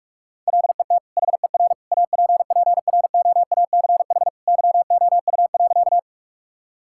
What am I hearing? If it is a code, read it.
Morse code, heard as a keyed tone, about 46 wpm, 700 Hz. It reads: LET HER APPROACH YOU.
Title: Sentences from Top 1000 Words - Review Part 6 - 45wpm